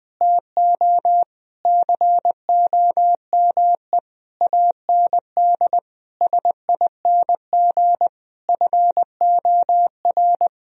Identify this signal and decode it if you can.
Morse code, heard as a keyed tone, about 20 wpm, 700 Hz. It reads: TO COME AND SING FOR
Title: Sentences from Top 1000 Words - Review Part 2 - 20wpm